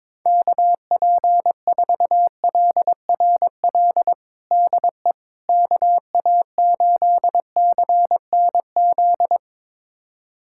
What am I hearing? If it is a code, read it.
Morse code, heard as a keyed tone, about 21 wpm, 700 Hz. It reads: KP4LRL DE KA8CNZ
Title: USA Callsign DE Callsign 22wpm